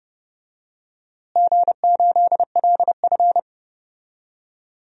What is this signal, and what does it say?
Morse code, heard as a keyed tone, about 30 wpm, 700 Hz. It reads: G8LF
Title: DXCC-100 Call Signs - 4 Characters - Repeated 3x - 30wpm